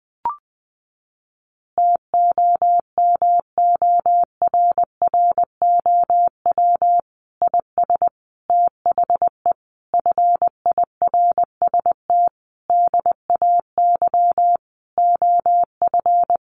Morse code, heard as a keyed tone, about 20 wpm, 700 Hz. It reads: TOMORROW IS THE FIRST DAY OF
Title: Additional Easy Phrases 20wpm